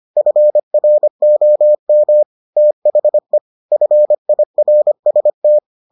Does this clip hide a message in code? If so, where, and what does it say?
Yes: Morse code from 0.2 to 5.6 s, heard as a keyed tone, about 25 wpm, 600 Hz. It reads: FROM THE FIRST